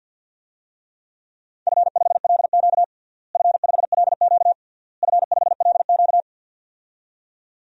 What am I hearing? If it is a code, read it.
Morse code, heard as a keyed tone, about 51 wpm, 700 Hz. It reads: F5LQ F5LQ F5LQ
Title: DXCC-100 Call Signs - 4 Characters - Repeated 3x - 50wpm